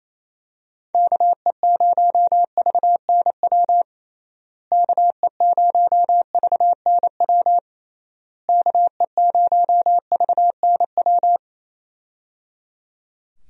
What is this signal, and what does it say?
Morse code, heard as a keyed tone, about 28 wpm, 700 Hz. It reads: KE0VNW KE0VNW KE0VNW
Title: USA Call Signs - 6 Characters - Repeated 3x - 28wpm